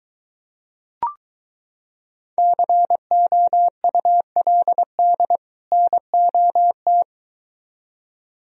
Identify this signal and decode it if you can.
Morse code, heard as a keyed tone, about 23 wpm, 700 Hz. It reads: COULD NOT